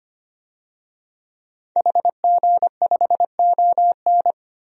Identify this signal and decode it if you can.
Morse code, heard as a keyed tone, about 25 wpm, 700 Hz. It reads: HG5ON
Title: DXCC-100 Call Signs - 5 Characters - 25wpm